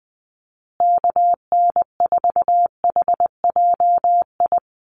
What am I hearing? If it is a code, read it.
Morse code, heard as a keyed tone, about 20 wpm, 700 Hz. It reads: KN4HJI
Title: USA Call Signs Repeated 2x 20wpm